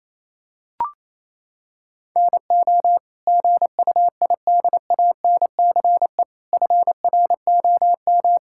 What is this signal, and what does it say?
Morse code, heard as a keyed tone, about 28 wpm, 700 Hz. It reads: NO GUIDANCE FROM
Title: Sentences from Top 1600 Words 28wpm